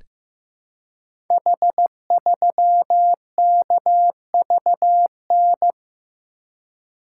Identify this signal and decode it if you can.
Morse code, heard as a keyed tone, about 15 wpm, 700 Hz. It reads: H3KVN
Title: Code Groups - 5 Letter:Numbers - 15wpm